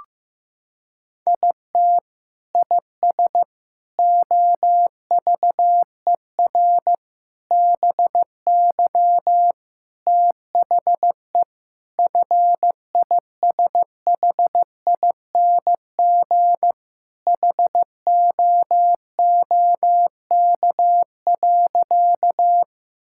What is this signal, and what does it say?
Morse code, heard as a keyed tone, about 15 wpm, 700 Hz. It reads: IT IS OVER BY THE FISHING HOOK.
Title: Sentences from Top 1600 Words 15wpm